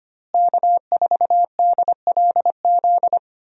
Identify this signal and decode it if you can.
Morse code, heard as a keyed tone, about 25 wpm, 700 Hz. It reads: K4DLZ
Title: USA Call Signs - 5 Characters - Repeated 3x - 25wpm